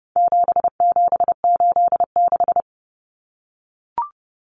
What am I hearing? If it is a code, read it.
Morse code, heard as a keyed tone, about 30 wpm, 700 Hz. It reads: 7786
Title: Numbers - 4-Digit 30wpm